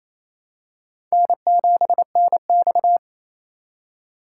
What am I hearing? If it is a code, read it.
Morse code, heard as a keyed tone, about 28 wpm, 700 Hz. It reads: N7NX
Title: USA Call Signs - 3x Character Spacing - 28wpm